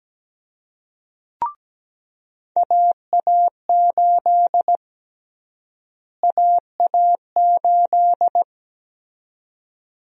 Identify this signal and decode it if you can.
Morse code, heard as a keyed tone, about 17 wpm, 700 Hz. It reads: AA8 AA8
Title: USA Call Sign Prefixes - Repeated 2x - 17wpm